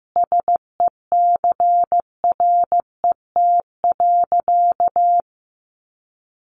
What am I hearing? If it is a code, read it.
Morse code, heard as a keyed tone, about 15 wpm, 700 Hz. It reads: SECRET.
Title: Sentences from Top 1200 Words 15wpm